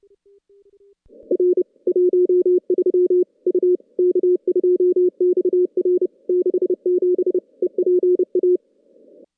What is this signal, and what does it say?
Morse code, heard as a keyed tone, about 30 wpm, 400 Hz. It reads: R 13 U K2XR 67 EPA
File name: contests / CW-SS-k5tr- -2100